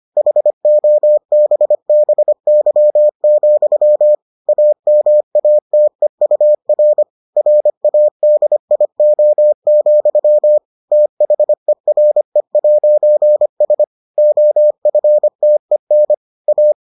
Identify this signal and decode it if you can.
Morse code, heard as a keyed tone, about 25 wpm, 600 Hz. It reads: HOBBY, AMATEUR RADIO, THERE'S OFTEN A